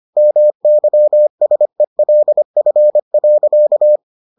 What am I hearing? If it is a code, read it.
Morse code, heard as a keyed tone, about 25 wpm, 600 Hz. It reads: MYSELF.